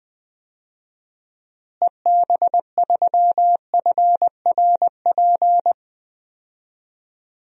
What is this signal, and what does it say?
Morse code, heard as a keyed tone, about 20 wpm, 690 Hz. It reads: EB3FRP
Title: DXCC-100 Call Signs - 6 Characters - Repeated 2x - 20wpm